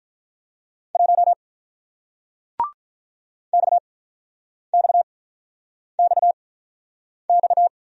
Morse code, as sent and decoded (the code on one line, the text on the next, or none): .---- / -..- -..- -..- -..-
1 XXXX